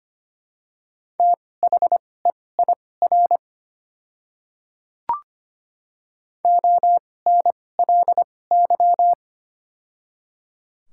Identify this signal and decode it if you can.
Morse code, heard as a keyed tone, about 25 wpm, spaced as for 18 wpm, 690 Hz. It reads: THEIR ONLY
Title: 2x Character Spacing - Top 300 Words - 25wpm